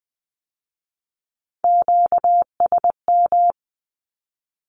Morse code, heard as a keyed tone, about 20 wpm, 700 Hz. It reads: QSM
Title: Q-Codes with Spoken Meaning - 20wpm